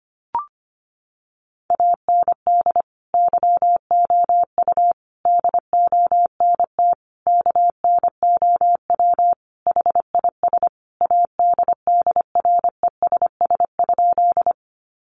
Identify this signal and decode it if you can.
Morse code, heard as a keyed tone, about 25 wpm, 700 Hz. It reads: AND YOU DONT KNOW HIS ADDRESS?
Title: Sentences from Top 1000 Words 25wpm